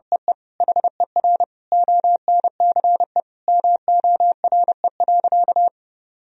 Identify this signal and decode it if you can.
Morse code, heard as a keyed tone, about 30 wpm, 700 Hz. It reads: EE HER ONCE MORE.
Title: Sentences from Top 500 Words and Abbreviations 30wpm